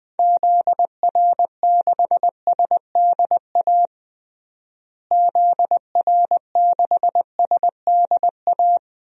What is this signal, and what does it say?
Morse code, heard as a keyed tone, about 21 wpm, 700 Hz. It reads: ZR6SDA ZR6SDA
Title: DXCC-100 Call Signs - 6 Characters - Repeated 3x - 20wpm